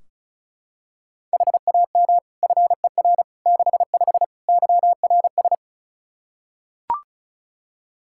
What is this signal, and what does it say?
Morse code, heard as a keyed tone, about 36 wpm, 700 Hz. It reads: HAM FER 65 YRS
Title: Practice QSO - Exchange 3 - More Info - 35wpm